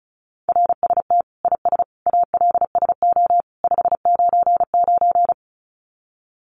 Morse code, heard as a keyed tone, about 34 wpm, 700 Hz. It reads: RST IS ALSO 599
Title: Common QSO Phrases 35wpm